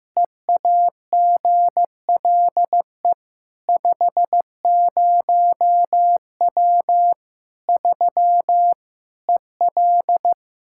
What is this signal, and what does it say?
Morse code, heard as a keyed tone, about 16 wpm, 700 Hz. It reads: EAGLE 50W 3 EL
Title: Practice QSO - Exchange 2 - Info - 15wpm